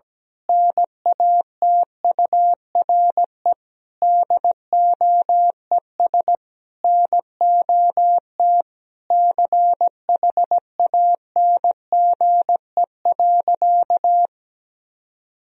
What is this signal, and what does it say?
Morse code, heard as a keyed tone, about 17 wpm, 700 Hz. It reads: NATURE DOES NOT CHANGE.
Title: Sentences from Top 1000 Words - Review Part 3 - 17wpm